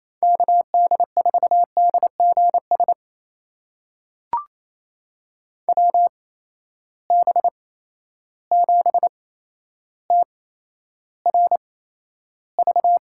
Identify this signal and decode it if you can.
Morse code, heard as a keyed tone, about 28 wpm, spaced as for 12 wpm, 700 Hz. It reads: KD4DGS WB7TRV
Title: USA Call Signs - 6 Characters - 8x Character Spacing - 28wpm